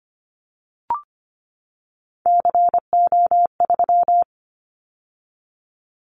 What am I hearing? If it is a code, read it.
Morse code, heard as a keyed tone, about 25 wpm, 700 Hz. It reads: CO3
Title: Callsign Prefixes for Top 100 DXCC - 25wpm